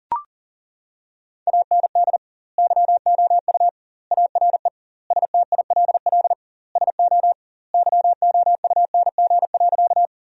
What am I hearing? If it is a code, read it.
Morse code, heard as a keyed tone, about 40 wpm, 700 Hz. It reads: AND YOU ARE STILL SO YOUNG.
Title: 2x Word-Spacing - Sentences from Top 400 Words - 40wpm